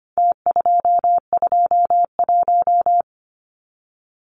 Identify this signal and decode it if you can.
Morse code, heard as a keyed tone, about 25 wpm, 700 Hz. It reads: T221
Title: Numbers - Contest Style - 4-Digit 25wpm